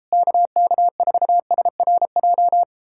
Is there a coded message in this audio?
Yes: Morse code from 0.1 to 2.6 s, heard as a keyed tone, about 37 wpm, 700 Hz. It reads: KK4SRJ